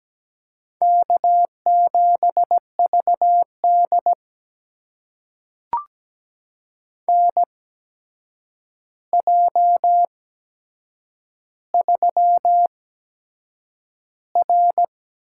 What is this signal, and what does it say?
Morse code, heard as a keyed tone, about 17 wpm, spaced as for 7 wpm, 700 Hz. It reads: K7VD NJ3R
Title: USA Call Signs - 4 Characters - 8x Character Spacing - 17wpm